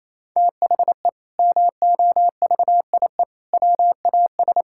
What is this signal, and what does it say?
Morse code, heard as a keyed tone, about 28 wpm, 700 Hz. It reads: THE MOVIE WAS
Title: Sentences from Top 1000 Words - Review Part 5 - 28wpm